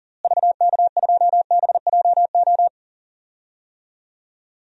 Morse code, heard as a keyed tone, about 40 wpm, 700 Hz. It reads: VK2BJO
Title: DXCC-100 Call Signs - 6 Characters - Repeated 2x - 40wpm